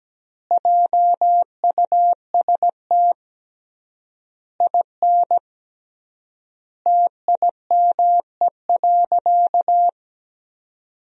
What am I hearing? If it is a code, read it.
Morse code, heard as a keyed tone, about 17 wpm, 700 Hz. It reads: JUST IN TIME.
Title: 3x Word-Spacing - Sentences from Top 300 Words - 17wpm